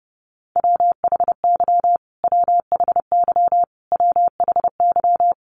Morse code, heard as a keyed tone, about 30 wpm, 700 Hz. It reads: WHY WHY WHY
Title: Top 200 Words Repeated 30wpm